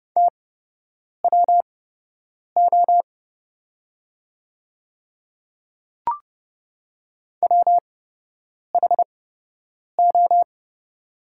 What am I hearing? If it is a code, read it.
Morse code, heard as a keyed tone, about 30 wpm, spaced as for 8 wpm, 700 Hz. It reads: TWO WHO
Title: 8x Character Spacing - Top 200 Words - 30wpm